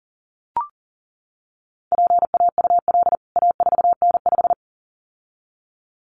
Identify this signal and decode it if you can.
Morse code, heard as a keyed tone, about 40 wpm, 700 Hz. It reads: PAUL A4N5
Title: CWT Exchange 40wpm